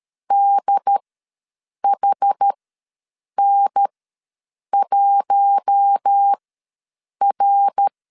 Morse code, heard as a keyed tone, about 13 wpm, spaced as for 7 wpm, 800 Hz. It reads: DHN1R
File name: LEZIONI / 5 LBVC / 5 L ws 12wpm 2958